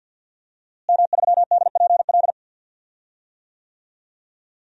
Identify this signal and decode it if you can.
Morse code, heard as a keyed tone, about 49 wpm, 700 Hz. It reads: M3DPL
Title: DXCC-100 Call Signs - 5 Characters - Repeated 3x - 50wpm